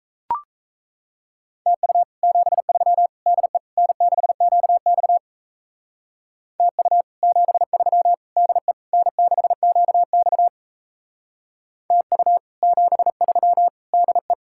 Morse code, heard as a keyed tone, about 33 wpm, 700 Hz. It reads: TU 73 DE N6QX TU 73 DE N6QX TU 73 DE
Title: Speed-Racing Common QSO Phrases Encore - 28wpm